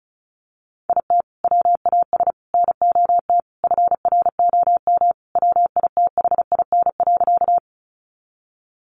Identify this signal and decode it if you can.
Morse code, heard as a keyed tone, about 34 wpm, 700 Hz. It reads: IT WAS NOT FROM WITHIN.
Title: Sentences from Top 700 Words 35wpm